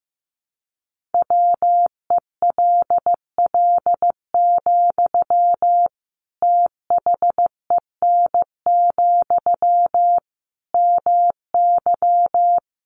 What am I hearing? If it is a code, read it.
Morse code, heard as a keyed tone, about 15 wpm, 700 Hz. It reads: WELL, THEN, MY
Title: Sentences from Top 1400 Words 15wpm